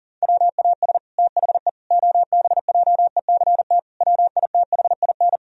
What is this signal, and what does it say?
Morse code, heard as a keyed tone, about 40 wpm, 700 Hz. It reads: WAS THE OBJECT WITHIN
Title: Sentences from Top 1000 Words - Review Part 4 - 40wpm